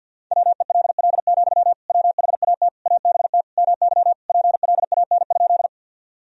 Morse code, heard as a keyed tone, about 52 wpm, 700 Hz. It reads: WELL, WHAT ABT MY PLAN?
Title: Sentences from Top 500 Words and Abbreviations 50wpm